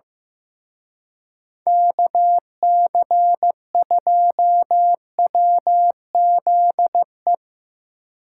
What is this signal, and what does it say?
Morse code, heard as a keyed tone, about 17 wpm, 700 Hz. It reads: KC2WZE
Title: Speed Racing - USA 2x3 Call Signs 15wpm